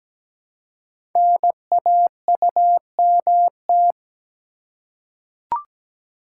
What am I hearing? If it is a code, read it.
Morse code, heard as a keyed tone, about 17 wpm, 700 Hz. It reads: NAUMT
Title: Code Groups - 5 Letters - 17wpm